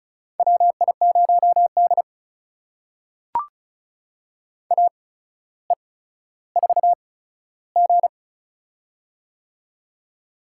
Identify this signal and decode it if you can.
Morse code, heard as a keyed tone, about 36 wpm, spaced as for 15 wpm, 700 Hz. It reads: WI0D AE4G
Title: USA Call Signs - 4 Characters - 8x Character Spacing - 35wpm